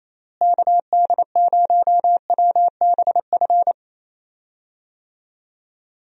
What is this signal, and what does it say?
Morse code, heard as a keyed tone, about 28 wpm, 700 Hz. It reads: KD0WBF
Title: USA Call Signs - 6 Characters - Repeated 3x - 28wpm